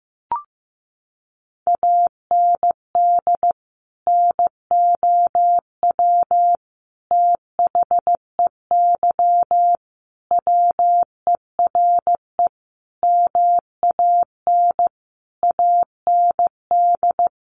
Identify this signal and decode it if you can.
Morse code, heard as a keyed tone, about 15 wpm, 700 Hz. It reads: AND NOW THEY WERE MAN AND
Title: Sentences from Top 1100 Words 15wpm